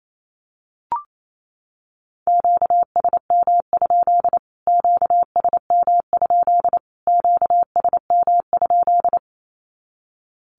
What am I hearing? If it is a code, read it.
Morse code, heard as a keyed tone, about 28 wpm, 700 Hz. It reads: QSM? QSM? QSM?
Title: Q-Codes Repeated 3x 28wpm